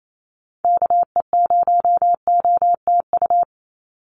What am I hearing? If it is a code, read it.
Morse code, heard as a keyed tone, about 28 wpm, 700 Hz. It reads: KE0OTU
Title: USA Call Signs 28wpm